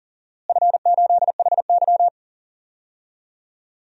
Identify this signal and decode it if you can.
Morse code, heard as a keyed tone, about 40 wpm, 700 Hz. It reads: F8HY
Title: DXCC-100 Call Signs - 4 Characters - Repeated 3x - 40wpm